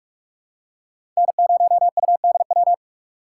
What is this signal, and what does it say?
Morse code, heard as a keyed tone, about 42 wpm, 700 Hz. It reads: N0UDW
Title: USA Call Signs - 5 Characters - 4x Character Spacing - 45wpm